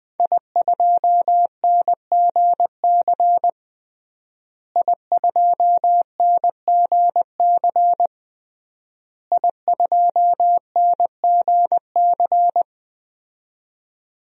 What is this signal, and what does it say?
Morse code, heard as a keyed tone, about 20 wpm, 700 Hz. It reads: I2NGC I2NGC I2NGC
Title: DXCC-100 Call Signs - 5 Characters - Repeated 3x - 20wpm